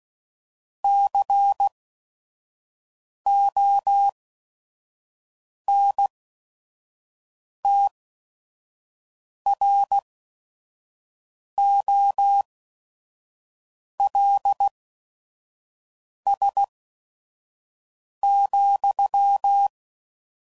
Morse code, heard as a keyed tone, about 16 wpm, spaced as for 5 wpm, 800 Hz. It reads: CONTROLS,
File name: CW Code Groups @5WPM